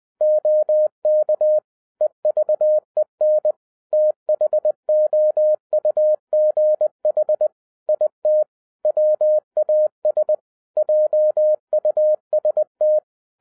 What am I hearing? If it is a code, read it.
Morse code, heard as a keyed tone, about 20 wpm, 610 Hz. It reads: OK EVEN THOUGH IT WAS JUST
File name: 2012-01-mp3 / n16-THROUGH EIGHT MONTHS-20wpm